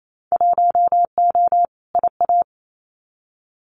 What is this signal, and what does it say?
Morse code, heard as a keyed tone, about 28 wpm, 700 Hz. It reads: 1O IA
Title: Winter Field Day Exchanges 28wpm